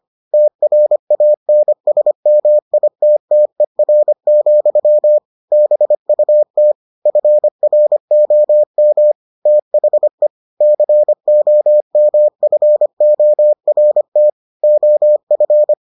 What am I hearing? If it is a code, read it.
Morse code, heard as a keyed tone, about 25 wpm, 600 Hz. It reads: TRANSMITTER, BUT FROM THE COMFORT OF